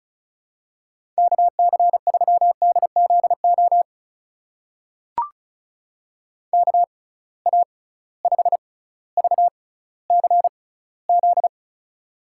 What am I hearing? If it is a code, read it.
Morse code, heard as a keyed tone, about 35 wpm, spaced as for 18 wpm, 700 Hz. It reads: KC3DZO KA5VCZ
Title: USA Call Signs - 6 Characters - 6x Character Spacing - 35wpm